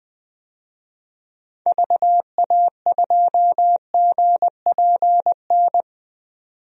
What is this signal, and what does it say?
Morse code, heard as a keyed tone, about 20 wpm, 700 Hz. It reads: VA2GPN
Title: DXCC-100 Call Signs - 6 Characters - 20wpm